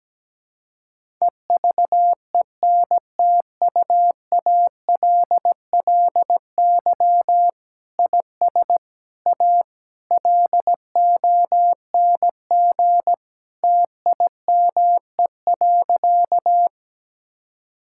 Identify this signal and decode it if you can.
Morse code, heard as a keyed tone, about 17 wpm, 700 Hz. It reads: EVENTUALLY IS A LONG TIME.
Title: Sentences from Top 1000 Words - Review Part 6 - 17wpm